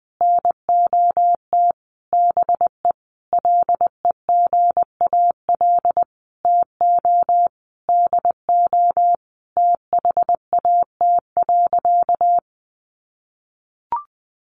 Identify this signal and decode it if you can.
Morse code, heard as a keyed tone, about 19 wpm, 700 Hz. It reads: NOT BE LEGAL TO DO THAT.
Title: Sentences from Top 1200 Words 20wpm